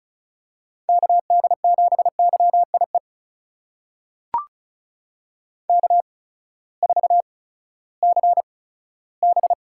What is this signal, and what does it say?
Morse code, heard as a keyed tone, about 36 wpm, spaced as for 18 wpm, 700 Hz. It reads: KD7YIE K4CB
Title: USA Call Signs - 8x Character Spacing - 35wpm